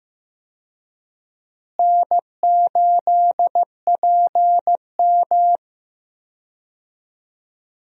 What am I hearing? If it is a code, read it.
Morse code, heard as a keyed tone, about 15 wpm, 700 Hz. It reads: N8PM
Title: USA Call Signs - 4 Characters - Repeated 2x - 15wpm